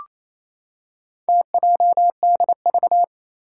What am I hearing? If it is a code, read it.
Morse code, heard as a keyed tone, about 29 wpm, 700 Hz. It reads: TJDV
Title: Four Challenging Letter-Number Combinations 28wpm